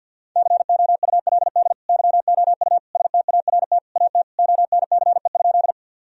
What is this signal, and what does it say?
Morse code, heard as a keyed tone, about 52 wpm, 700 Hz. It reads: COULD YOU START AT ONCE?
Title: Sentences from Top 1000 Words - Review Part 1 - 50wpm